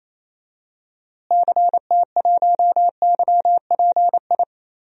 Morse code, heard as a keyed tone, about 28 wpm, 700 Hz. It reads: CT1YPI
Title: DXCC-100 Call Signs - 6 Characters - Repeated 3x - 28wpm